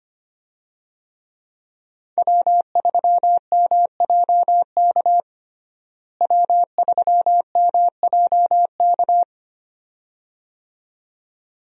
Morse code, heard as a keyed tone, about 25 wpm, 700 Hz. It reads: W3MJK W3MJK
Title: USA Call Signs - 5 Characters - Repeated 2x - 25wpm